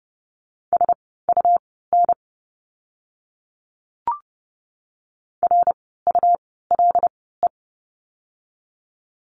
Morse code, heard as a keyed tone, about 30 wpm, spaced as for 17 wpm, 700 Hz. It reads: SUN RULE
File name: Lesson 17f - Letter u - Words 30wpm